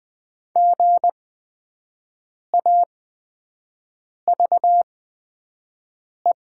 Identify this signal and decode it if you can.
Morse code, heard as a keyed tone, about 20 wpm, spaced as for 5 wpm, 700 Hz. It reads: GAVE
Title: 8x Character Spacing - Top 500 Words - 20wpm